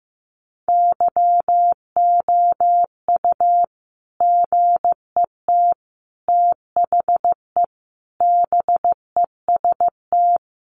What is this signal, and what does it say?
Morse code, heard as a keyed tone, about 15 wpm, 700 Hz. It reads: YOU GET THE BEST
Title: Sentences from Top 1200 Words 15wpm